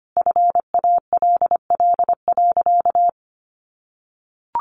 Morse code, heard as a keyed tone, about 25 wpm, 700 Hz. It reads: FALL.
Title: Sentences from Top 500 Words 25wpm